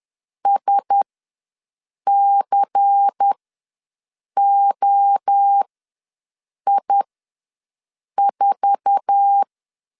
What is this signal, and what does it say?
Morse code, heard as a keyed tone, about 10 wpm, spaced as for 6 wpm, 800 Hz. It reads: SCOI4
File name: LEZIONI / 10 SIGNS / 10A SLASH ws 10wpm 4121